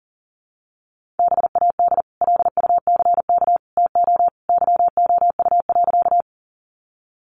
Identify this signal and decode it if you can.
Morse code, heard as a keyed tone, about 40 wpm, 700 Hz. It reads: BAD LUCK TO YOU.
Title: Sentences from Top 1500 Words 40wpm